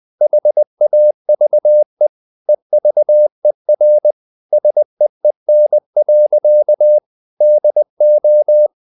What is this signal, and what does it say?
Morse code, heard as a keyed tone, about 20 wpm, 600 Hz. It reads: HAVE EVER SEEN. DO
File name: the old man and the sea / the old man and the sea part 1 20wpm